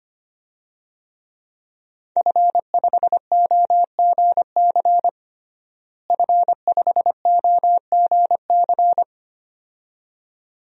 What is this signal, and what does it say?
Morse code, heard as a keyed tone, about 25 wpm, 700 Hz. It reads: F5OGC F5OGC
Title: DXCC-100 Call Signs - 5 Characters - Repeated 2x - 25wpm